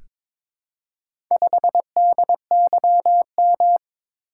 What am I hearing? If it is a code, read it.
Morse code, heard as a keyed tone, about 22 wpm, 700 Hz. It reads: 5DYM
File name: practice / 22wpm / Four Letter-Number Combinations - 22wpm